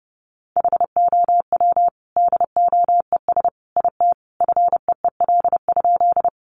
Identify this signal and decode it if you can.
Morse code, heard as a keyed tone, about 30 wpm, 700 Hz. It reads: HOW DOES IT FEEL?
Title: Sentences from Top 400 Words 30wpm